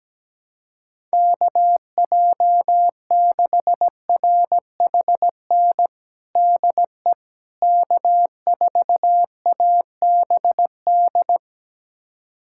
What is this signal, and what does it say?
Morse code, heard as a keyed tone, about 17 wpm, 700 Hz. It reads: KJ6RHN DE K4ABD
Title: USA Callsign DE Callsign 17wpm